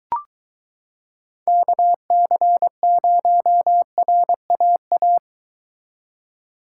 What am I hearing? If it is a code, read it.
Morse code, heard as a keyed tone, about 22 wpm, 700 Hz. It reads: KC0RAA